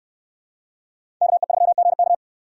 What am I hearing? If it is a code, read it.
Morse code, heard as a keyed tone, about 68 wpm, 700 Hz. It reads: KE3GK